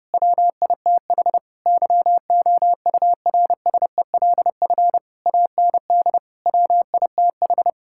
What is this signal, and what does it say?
Morse code, heard as a keyed tone, about 28 wpm, 700 Hz. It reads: WITH YOURSELF AND WITH
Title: Sentences from Top 1000 Words - Review Part 6 - 30wpm